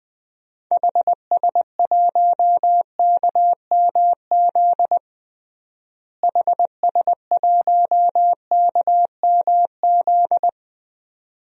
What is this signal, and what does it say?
Morse code, heard as a keyed tone, about 20 wpm, 700 Hz. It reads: HS1KMZ HS1KMZ
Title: DXCC-100 Call Signs - 6 Characters - Repeated 2x - 20wpm